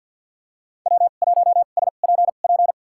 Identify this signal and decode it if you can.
Morse code, heard as a keyed tone, about 49 wpm, spaced as for 36 wpm, 700 Hz. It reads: W1SPP
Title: USA Call Signs - 2x Character Spacing - 50wpm